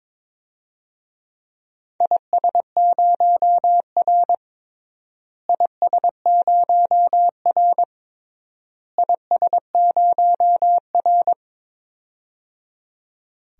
Morse code, heard as a keyed tone, about 22 wpm, 700 Hz. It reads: IS0R IS0R IS0R
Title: DXCC-100 Call Signs - 4 Characters - Repeated 3x - 22wpm